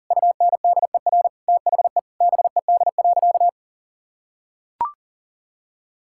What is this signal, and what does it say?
Morse code, heard as a keyed tone, about 40 wpm, 700 Hz. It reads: UNDER THE BED.
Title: Sentences from Top 1000 Words - Review Part 5 - 40wpm